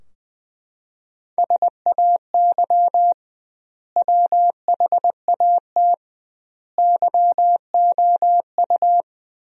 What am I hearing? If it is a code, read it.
Morse code, heard as a keyed tone, about 21 wpm, 700 Hz. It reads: SAY WHAT YOU
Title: 2x Word-Spacing - Sentences from Top 200 Words - 20wpm